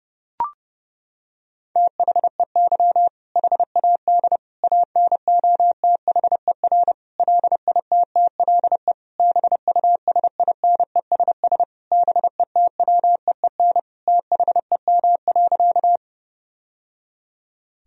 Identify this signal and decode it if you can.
Morse code, heard as a keyed tone, about 32 wpm, 700 Hz. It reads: THEY HAD ANOTHER LITTLE BUSINESS BETWEEN THEM.
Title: Sentences from Top 1000 Words - Review Part 1 - 30wpm